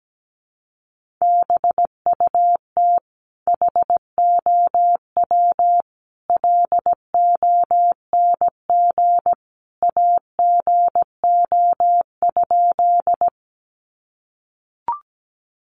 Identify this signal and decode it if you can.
Morse code, heard as a keyed tone, about 17 wpm, 700 Hz. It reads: BUT HOW LONG AGO?
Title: Sentences from Top 500 Words 17wpm